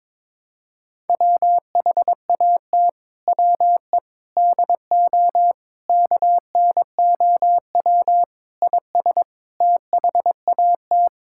Morse code, heard as a keyed tone, about 22 wpm, 700 Hz. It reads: WHAT WE DO KNOW IS THAT
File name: practice / 22wpm / Sets of 7 Words - 22wpm